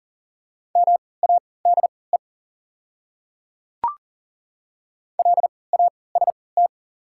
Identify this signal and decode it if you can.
Morse code, heard as a keyed tone, about 40 wpm, spaced as for 23 wpm, 700 Hz. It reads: MADE LAST
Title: 3x Character Spacing - Top 300 Words - 40wpm